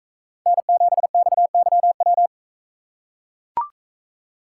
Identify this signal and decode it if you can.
Morse code, heard as a keyed tone, about 44 wpm, 700 Hz. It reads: N7XYW